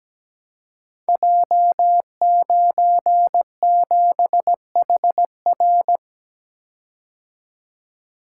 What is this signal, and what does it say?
Morse code, heard as a keyed tone, about 17 wpm, 700 Hz. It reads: J97HR
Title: Code Groups - 5 Letter:Numbers - 17wpm